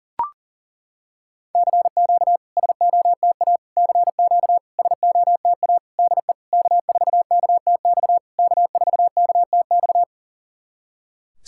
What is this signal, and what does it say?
Morse code, heard as a keyed tone, about 39 wpm, 690 Hz. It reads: CQ SOTA CQ SOTA DE K4KTX K4KTX
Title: SOTA QSO Practice 40wpm